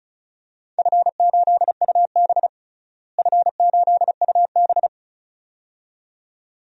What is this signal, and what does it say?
Morse code, heard as a keyed tone, about 36 wpm, 700 Hz. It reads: F8UB F8UB
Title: DXCC-100 Call Signs - 4 Characters - Repeated 2x - 35wpm